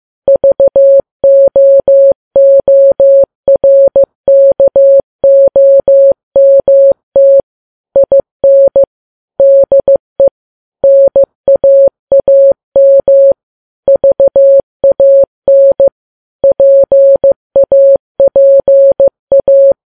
Morse code, heard as a keyed tone, about 15 wpm, 550 Hz. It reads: VOORKOMT IN DE NAAM VAN PAPA
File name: Files Web tls / Morse on9fv / morse15kps